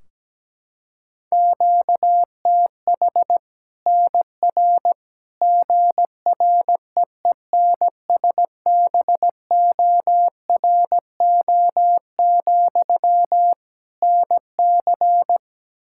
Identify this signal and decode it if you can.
Morse code, heard as a keyed tone, about 17 wpm, 700 Hz. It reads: QTH NR GREENSBORO, NC
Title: Practice QSO - Exchange 1 - Beginning - 17wpm